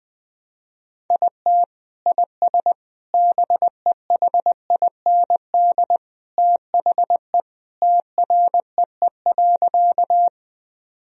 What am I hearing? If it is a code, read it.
Morse code, heard as a keyed tone, about 20 wpm, 700 Hz. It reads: IT IS BEHIND THE TREE.
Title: Sentences from Top 1000 Words - Review Part 1 - 20wpm